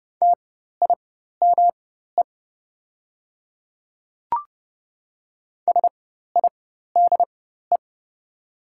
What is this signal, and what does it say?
Morse code, heard as a keyed tone, about 30 wpm, spaced as for 14 wpm, 700 Hz. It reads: TIME SIDE
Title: 4x Character Spacing - Top 300 Words - 30wpm